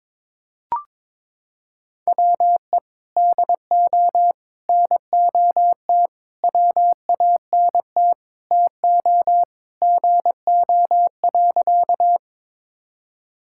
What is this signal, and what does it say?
Morse code, heard as a keyed tone, about 21 wpm, 700 Hz. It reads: WE DO NOT WANT TO GO.
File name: practice / 22wpm / Sentences with 4-Word Phrases - 22wpm